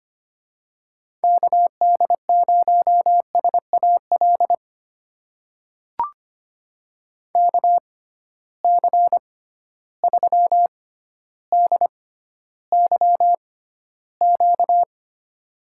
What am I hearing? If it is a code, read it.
Morse code, heard as a keyed tone, about 25 wpm, spaced as for 13 wpm, 700 Hz. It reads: KD0SAL KC3DYQ
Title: USA Call Signs - 6 Characters - 6x Character Spacing - 25wpm